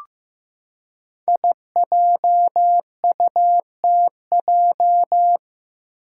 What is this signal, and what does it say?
Morse code, heard as a keyed tone, about 15 wpm, 700 Hz. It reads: IJUTJ
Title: Five Challenging Letter-Number Combinations 15wpm